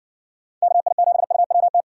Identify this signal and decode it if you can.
Morse code, heard as a keyed tone, about 60 wpm, 700 Hz. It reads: KI7UWT